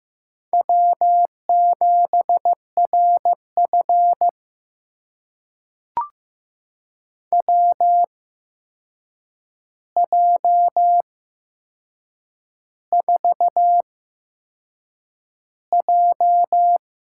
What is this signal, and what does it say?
Morse code, heard as a keyed tone, about 15 wpm, spaced as for 6 wpm, 700 Hz. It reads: W7RF WJ4J
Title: USA Call Signs - 4 Characters - 8x Character Spacing - 15wpm